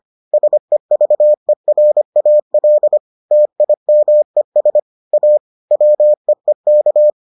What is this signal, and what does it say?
Morse code, heard as a keyed tone, about 25 wpm, 600 Hz. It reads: SEVERAL TIMES A WEEK